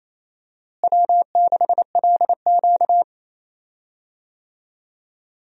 Morse code, heard as a keyed tone, about 28 wpm, 700 Hz. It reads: W6LQ
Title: USA Call Signs - 4 Characters - Repeated 3x - 28wpm